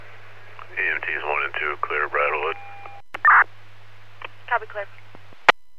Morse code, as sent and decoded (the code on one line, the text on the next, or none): none